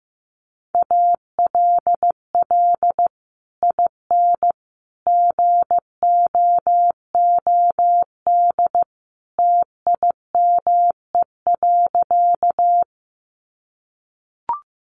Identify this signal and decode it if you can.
Morse code, heard as a keyed tone, about 15 wpm, 700 Hz. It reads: ALL IN GOOD TIME.
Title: Sentences from Top 200 Words 15wpm